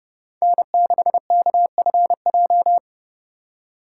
Morse code, heard as a keyed tone, about 31 wpm, 700 Hz. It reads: N6KFJ
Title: USA Call Signs - 3x Character Spacing - 30wpm